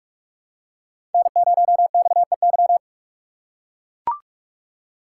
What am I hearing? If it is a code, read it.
Morse code, heard as a keyed tone, about 44 wpm, 690 Hz. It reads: N0XEY
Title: USA Call Signs - 4x Character Spacing - 45wpm